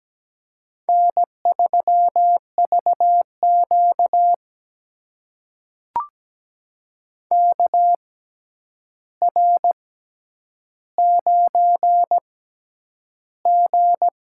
Morse code, heard as a keyed tone, about 17 wpm, spaced as for 9 wpm, 700 Hz. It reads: N3VQ KR9G
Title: USA Call Signs - 4 Characters - 6x Character Spacing - 17wpm